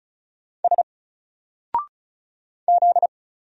Morse code, heard as a keyed tone, about 34 wpm, 700 Hz. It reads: S Z
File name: Single-Letter-Number-35wpm